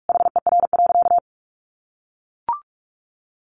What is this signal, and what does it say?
Morse code, heard as a keyed tone, about 44 wpm, 700 Hz. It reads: HER.
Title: Sentences from Top 700 Words 45wpm